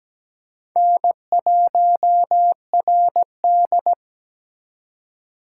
Morse code, heard as a keyed tone, about 17 wpm, 700 Hz. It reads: N1RD
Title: USA Call Signs - 4 Characters - 2x Character Spacing - 17wpm